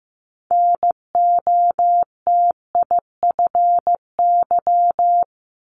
Morse code, heard as a keyed tone, about 15 wpm, 700 Hz. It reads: NOTIFY
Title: Multi-letter Phonemes 15wpm